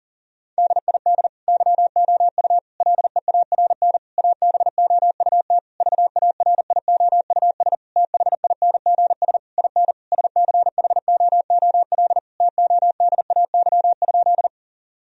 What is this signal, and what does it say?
Morse code, heard as a keyed tone, about 40 wpm, 700 Hz. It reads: DID YOU LEARN ABOUT VARIOUS THINGS IN SCHOOL TODAY?
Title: Sentences from Top 1000 Words - Review Part 4 - 40wpm